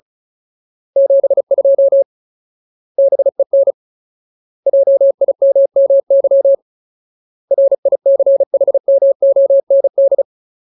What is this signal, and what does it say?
Morse code, heard as a keyed tone, about 34 wpm, 550 Hz. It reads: 72 BEN JIMMY RICHMOND
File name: Ultimate Word List / 35WPM Ultimate